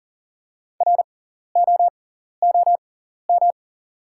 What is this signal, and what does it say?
Morse code, heard as a keyed tone, about 38 wpm, spaced as for 14 wpm, 690 Hz. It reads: ROOM
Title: 6x Character Spacing - Top 300 Words - 40wpm